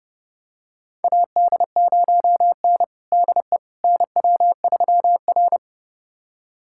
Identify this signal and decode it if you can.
Morse code, heard as a keyed tone, about 31 wpm, 700 Hz. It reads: AD0N DE NW3R
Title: USA Callsign DE Callsign - 4 Characters - 30wpm